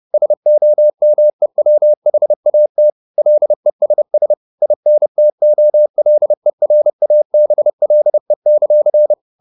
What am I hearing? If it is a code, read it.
Morse code, heard as a keyed tone, about 30 wpm, 600 Hz. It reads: SOMEWHAT LESS INTOLERABLE;
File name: the old man and the sea / the old man and the sea 1 30wpm